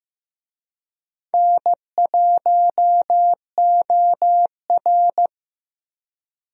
Morse code, heard as a keyed tone, about 15 wpm, 700 Hz. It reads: N1OR
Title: USA Call Signs - 4 Characters - Repeated 2x - 15wpm